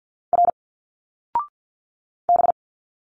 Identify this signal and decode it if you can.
Morse code, heard as a keyed tone, about 50 wpm, 700 Hz. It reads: R B
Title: Single Letters 50wpm